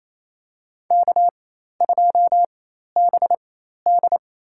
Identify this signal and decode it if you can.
Morse code, heard as a keyed tone, about 28 wpm, spaced as for 13 wpm, 700 Hz. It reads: K2BD
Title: USA Call Signs - 4 Characters - 4x Character Spacing - 28wpm